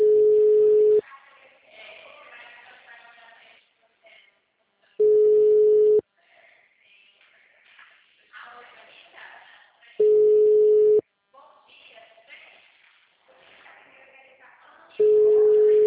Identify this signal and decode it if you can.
Ringback tone (Europe, 425 Hz), with speech.